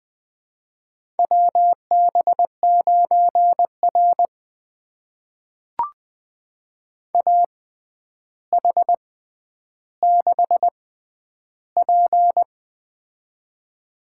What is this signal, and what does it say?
Morse code, heard as a keyed tone, about 20 wpm, spaced as for 10 wpm, 700 Hz. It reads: WB9R AH6P
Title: USA Call Signs - 4 Characters - 6x Character Spacing - 20wpm